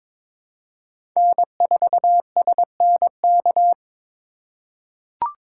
Morse code, heard as a keyed tone, about 22 wpm, 700 Hz. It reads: N4SNK
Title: USA Call Signs - 5 Characters - 6x Character Spacing - 22wpm